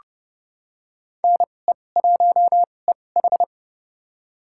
Morse code, heard as a keyed tone, about 29 wpm, spaced as for 21 wpm, 700 Hz. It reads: NE1EH
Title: USA Call Signs - 5 Characters - 2x Character Spacing - 30wpm